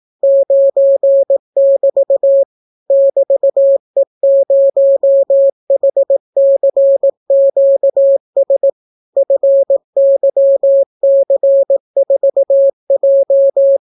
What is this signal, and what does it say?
Morse code, heard as a keyed tone, about 18 wpm, 550 Hz. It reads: 9<BT> <BT>E0HCQS FYC4J